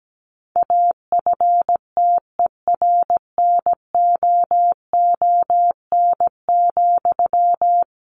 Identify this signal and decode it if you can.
Morse code, heard as a keyed tone, about 17 wpm, 700 Hz. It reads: AFTERNOON,
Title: Sentences from Top 1500 Words 17wpm